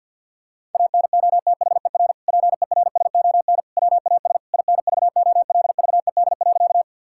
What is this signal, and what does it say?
Morse code, heard as a keyed tone, about 52 wpm, 700 Hz. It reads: ANOTHER PERSON WAS INVOLVED.
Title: Sentences from Top 1000 Words - Review Part 5 - 50wpm